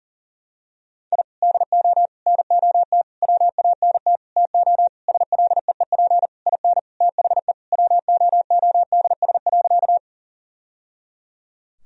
Morse code, heard as a keyed tone, about 37 wpm, 700 Hz. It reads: I DO NOT WANT TO SLEEP IN THE WOODS.
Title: Sentences from Top 1000 Words - Review Part 2 - 40wpm